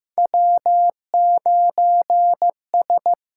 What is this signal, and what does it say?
Morse code, heard as a keyed tone, about 15 wpm, 700 Hz. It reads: W9S
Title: USA Call Signs - 4 Characters - 6x Character Spacing - 15wpm